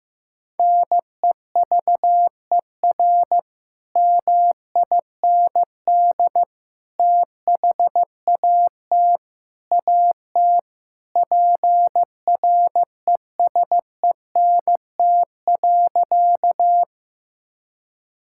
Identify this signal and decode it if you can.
Morse code, heard as a keyed tone, about 15 wpm, 700 Hz. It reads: NEVER MIND THAT AT PRESENT.
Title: Sentences from Top 1000 Words - Review Part 5 - 15wpm